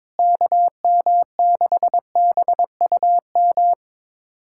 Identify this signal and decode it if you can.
Morse code, heard as a keyed tone, about 22 wpm, 700 Hz. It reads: KM6BUM
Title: USA Callsign DE Callsign 22wpm